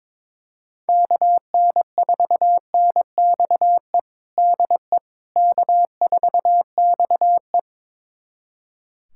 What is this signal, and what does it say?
Morse code, heard as a keyed tone, about 22 wpm, 700 Hz. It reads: KN4NXE DE K4XE
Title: USA Callsign DE Callsign 22wpm